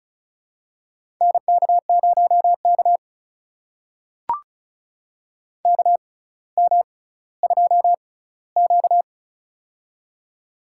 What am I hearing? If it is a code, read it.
Morse code, heard as a keyed tone, about 36 wpm, spaced as for 18 wpm, 700 Hz. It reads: NK0K KM2Q
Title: USA Call Signs - 4 Characters - 6x Character Spacing - 35wpm